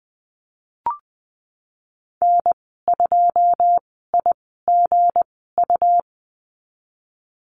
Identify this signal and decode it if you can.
Morse code, heard as a keyed tone, about 20 wpm, spaced as for 15 wpm, 700 Hz. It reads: N2IGU
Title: 161 Lesson 26g - Letter g - Call Signs 20wpm